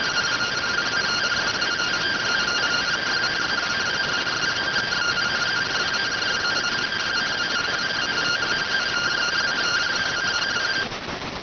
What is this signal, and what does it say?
Two-tone teleprinter signal, about 170 Hz shift.